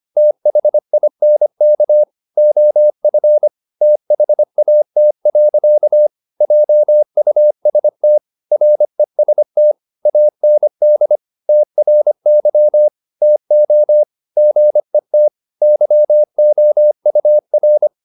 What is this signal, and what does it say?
Morse code, heard as a keyed tone, about 25 wpm, 600 Hz. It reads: THINK OF THAT. JUST REST AND TRY TO GET YOUR